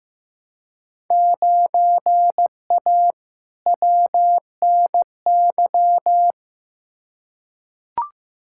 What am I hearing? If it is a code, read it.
Morse code, heard as a keyed tone, about 15 wpm, 700 Hz. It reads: 9A WNY
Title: Field Day Exchanges Encore - 15wpm